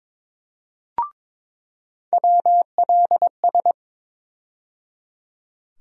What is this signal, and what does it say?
Morse code, heard as a keyed tone, about 22 wpm, 700 Hz. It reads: WLS
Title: USA Call Sign Postfixes - 3 Letters 22wpm